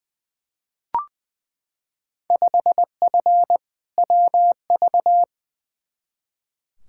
Morse code, heard as a keyed tone, about 20 wpm, 700 Hz. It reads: 5F WV
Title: Field Day Exchanges Encore - 20wpm